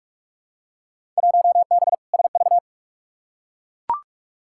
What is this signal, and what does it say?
Morse code, heard as a keyed tone, about 43 wpm, 700 Hz. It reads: 1B SV
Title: Field Day Exchanges Encore - 45wpm